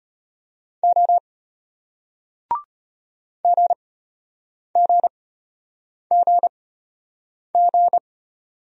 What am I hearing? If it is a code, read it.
Morse code, heard as a keyed tone, about 33 wpm, spaced as for 8 wpm, 700 Hz. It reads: O GGGG